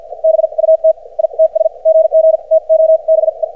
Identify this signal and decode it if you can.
Morse code, heard as a keyed tone, about 50 wpm, spaced as for 42 wpm, 590 Hz. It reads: BUT ITS GG TO BE
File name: archived / CRSnet